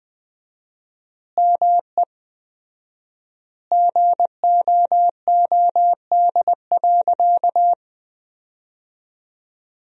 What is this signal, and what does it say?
Morse code, heard as a keyed tone, about 20 wpm, 700 Hz. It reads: ME GOOD.
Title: 4x Word-Spacing - Sentences from Top 500 Words - 20wpm